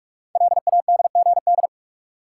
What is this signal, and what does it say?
Morse code, heard as a keyed tone, about 44 wpm, 700 Hz. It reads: LADGD
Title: Code Groups - 5 Letters - 45wpm